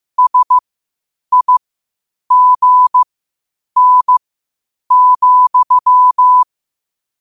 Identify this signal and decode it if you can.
Morse code, heard as a keyed tone, about 15 wpm, spaced as for 8 wpm, 1000 Hz. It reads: SIGN,
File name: qotdfiles / 8wpm